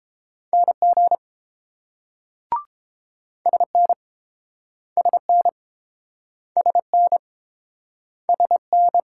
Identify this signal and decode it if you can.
Morse code, heard as a keyed tone, about 30 wpm, 700 Hz. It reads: NG SN SN SN SN